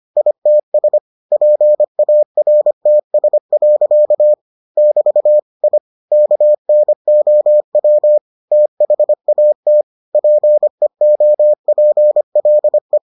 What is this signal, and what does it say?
Morse code, heard as a keyed tone, about 25 wpm, 600 Hz. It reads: ITS PARTS. <BT> I KNOW THAT PEOPLE